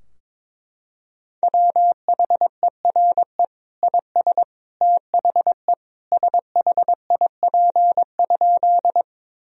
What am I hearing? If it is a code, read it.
Morse code, heard as a keyed tone, about 22 wpm, 700 Hz. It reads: WHERE IS THE SHIP?
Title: Sentences from Top 1000 Words - Review Part 2 - 22wpm